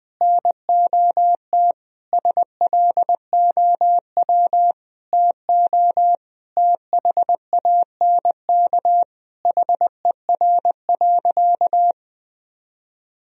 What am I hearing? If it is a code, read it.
Morse code, heard as a keyed tone, about 20 wpm, 700 Hz. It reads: NOT SLOW TO THANK HER.
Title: Sentences from Top 1000 Words - Review Part 2 - 20wpm